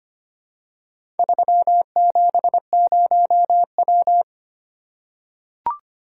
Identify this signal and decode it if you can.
Morse code, heard as a keyed tone, about 25 wpm, 700 Hz. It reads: 370W
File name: practice / 25wpm / Four Letter-Number Combinations - 25wpm